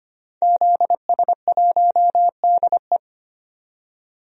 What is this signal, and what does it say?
Morse code, heard as a keyed tone, about 25 wpm, 700 Hz. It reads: ZS1DE
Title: DXCC-100 Call Signs - 5 Characters - 25wpm